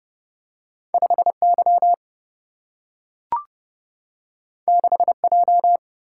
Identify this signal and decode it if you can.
Morse code, heard as a keyed tone, about 30 wpm, 700 Hz. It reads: 5 Y 6 J
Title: Single-Letter Single-Number 30wpm